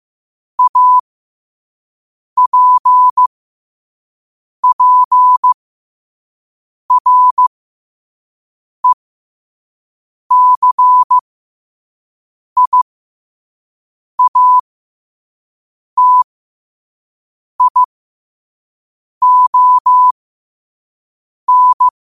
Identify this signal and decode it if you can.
Morse code, heard as a keyed tone, about 15 wpm, spaced as for 5 wpm, 1000 Hz. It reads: APPRECIATION
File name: qotdfiles / 5wpm